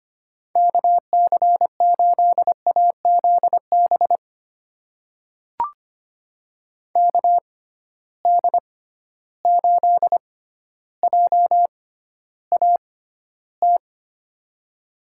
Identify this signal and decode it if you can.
Morse code, heard as a keyed tone, about 25 wpm, spaced as for 13 wpm, 700 Hz. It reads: KC8AZB KD8JAT
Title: USA Call Signs - 6 Characters - 6x Character Spacing - 25wpm